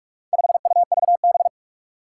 Morse code, heard as a keyed tone, about 46 wpm, 700 Hz. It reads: 5UVB
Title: Four Challenging Letter-Number Combinations 45wpm